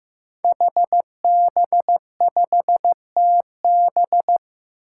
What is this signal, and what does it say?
Morse code, heard as a keyed tone, about 15 wpm, 700 Hz. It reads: HB5TB
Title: Five Challenging Letter-Number Combinations 15wpm